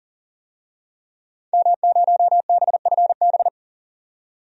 Morse code, heard as a keyed tone, about 40 wpm, 700 Hz. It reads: M0BFB
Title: DXCC-100 Call Signs - 5 Characters - 40wpm